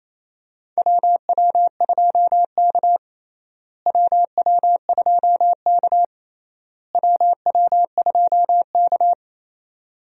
Morse code, heard as a keyed tone, about 28 wpm, 700 Hz. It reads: WW2K WW2K WW2K
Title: USA Call Signs - 4 Characters - Repeated 3x - 28wpm